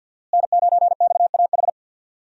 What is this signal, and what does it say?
Morse code, heard as a keyed tone, about 46 wpm, 700 Hz. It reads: N9XAH
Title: USA Call Signs - 5 Characters - Repeated 2x - 50wpm